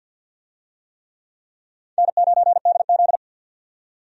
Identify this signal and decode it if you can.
Morse code, heard as a keyed tone, about 47 wpm, 700 Hz. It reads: N9DZ
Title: USA Call Signs - 4 Characters - 50wpm